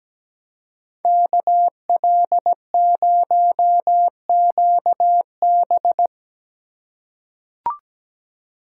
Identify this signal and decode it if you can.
Morse code, heard as a keyed tone, about 17 wpm, 700 Hz. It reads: KL0QB
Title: USA Call Signs - 5 Characters - 4x Character Spacing - 17wpm